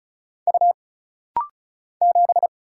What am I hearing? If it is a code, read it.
Morse code, heard as a keyed tone, about 36 wpm, 700 Hz. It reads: U 7
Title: Single Letter-Number - Rapid-Fire 35wpm